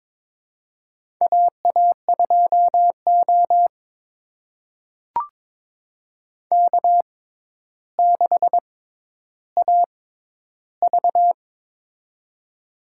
Morse code, heard as a keyed tone, about 22 wpm, spaced as for 11 wpm, 700 Hz. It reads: AA2O K6AV
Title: USA Call Signs - 4 Characters - 6x Character Spacing - 22wpm